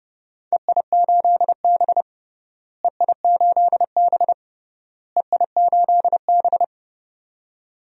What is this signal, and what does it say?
Morse code, heard as a keyed tone, about 30 wpm, 700 Hz. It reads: EI8B EI8B EI8B
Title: DXCC-100 Call Signs - 4 Characters - Repeated 3x - 30wpm